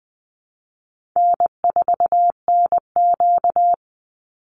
Morse code, heard as a keyed tone, about 20 wpm, 700 Hz. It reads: N4NQ
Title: Call Signs Repeated 3x - 20wpm